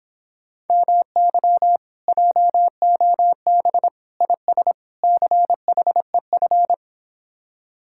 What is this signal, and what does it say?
Morse code, heard as a keyed tone, about 26 wpm, 700 Hz. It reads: MY JOB IS CHEF
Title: Speed-Racing Common QSO Phrases - 22wpm